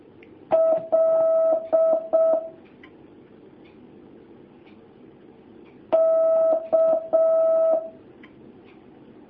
Morse code, heard as a keyed tone, about 6 wpm, 650 Hz. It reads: L K
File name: jh8cbh / 2019cw